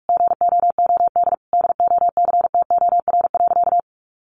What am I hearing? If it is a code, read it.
Morse code, heard as a keyed tone, about 44 wpm, 700 Hz. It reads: GOOD DOCTOR.
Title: Sentences from Top 1500 Words 45wpm